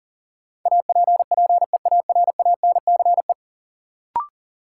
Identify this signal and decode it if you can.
Morse code, heard as a keyed tone, about 40 wpm, 700 Hz. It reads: APPEARANCE
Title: Sentences from Top 1100 Words - New Word Drill Alt - 40wpm